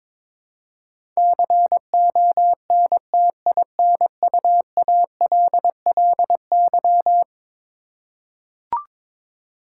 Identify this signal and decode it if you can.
Morse code, heard as a keyed tone, about 22 wpm, 700 Hz. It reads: CONTINUALLY
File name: practice / 22wpm / Eleven Letter Words - 22wpm